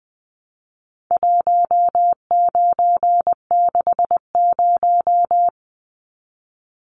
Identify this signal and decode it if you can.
Morse code, heard as a keyed tone, about 20 wpm, 700 Hz. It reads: 1960
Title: Numbers - 4-Digit 20wpm